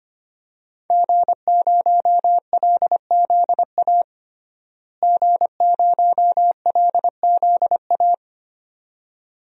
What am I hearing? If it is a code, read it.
Morse code, heard as a keyed tone, about 25 wpm, 700 Hz. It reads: G0LZA G0LZA
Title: DXCC-100 Call Signs - 5 Characters - Repeated 2x - 25wpm